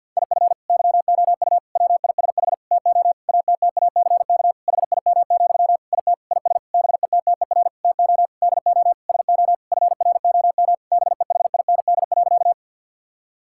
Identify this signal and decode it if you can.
Morse code, heard as a keyed tone, about 50 wpm, 700 Hz. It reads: IF YOU WISH TO ATTACK HIM, IT IS BETTER TO DO SO FROM BEHIND.
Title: Sentences from Top 1600 Words 50wpm